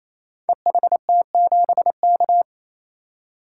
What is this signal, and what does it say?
Morse code, heard as a keyed tone, about 27 wpm, 700 Hz. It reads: EHT7K
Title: Code Groups - 5 Letter:Numbers - 28wpm